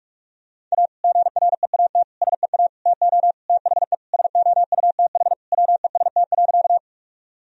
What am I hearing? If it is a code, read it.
Morse code, heard as a keyed tone, about 41 wpm, 700 Hz. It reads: A GREAT SEA TO THE SOUTH WEST.
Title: Sentences from Top 1000 Words - Review Part 2 - 45wpm